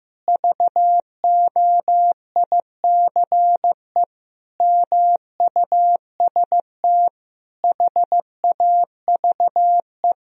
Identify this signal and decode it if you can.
Morse code, heard as a keyed tone, about 16 wpm, 700 Hz. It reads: VOICE MUST HAVE
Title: Sentences from Top 1000 Words - Review Part 2 - 15wpm